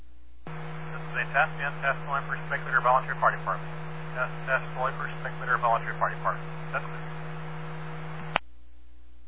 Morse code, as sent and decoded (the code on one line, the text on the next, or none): none